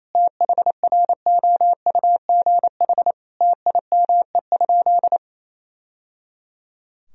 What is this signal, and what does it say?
Morse code, heard as a keyed tone, about 28 wpm, 700 Hz. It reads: THROUGH TIME?
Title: Sentences from Top 1000 Words - Review Part 2 - 28wpm